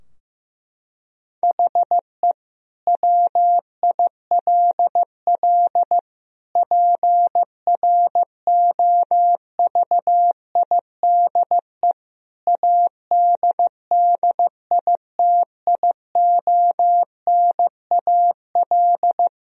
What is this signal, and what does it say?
Morse code, heard as a keyed tone, about 15 wpm, 700 Hz. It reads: HE WILL PROVIDE ADDITIONAL
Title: Sentences from Top 1000 Words - Review Part 4 - 15wpm